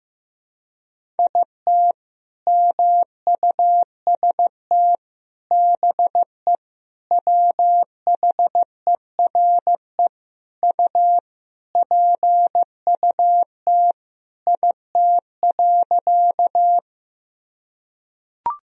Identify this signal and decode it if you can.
Morse code, heard as a keyed tone, about 15 wpm, 700 Hz. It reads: IT MUST BE WHERE U PUT IT.
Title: Sentences from Top 200 Words and Abbreviations 15wpm